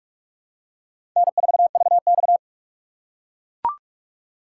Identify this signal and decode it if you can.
Morse code, heard as a keyed tone, about 41 wpm, 700 Hz. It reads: N4VX
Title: USA Call Signs - 4 Characters - 2x Character Spacing - 45wpm